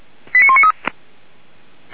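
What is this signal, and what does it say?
Five-tone selective call (ZVEI), heard as quick runs of five tones, tones of about 70 ms. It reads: 89193
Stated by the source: ZVEI five-tone selective call received with an SDR.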